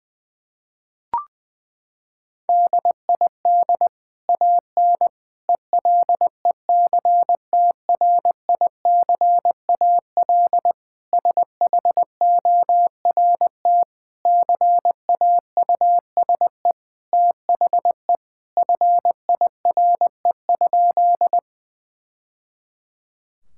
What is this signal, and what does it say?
Morse code, heard as a keyed tone, about 21 wpm, 700 Hz. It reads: DID AN ELECTRICAL SHORT CAUSE THE FIRE?
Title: Sentences from Top 1000 Words - Review Part 6 - 20wpm